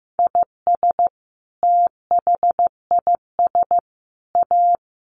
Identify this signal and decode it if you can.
Morse code, heard as a keyed tone, about 15 wpm, 700 Hz. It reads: IS THIS A
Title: Sentences from Top 900 Words 15wpm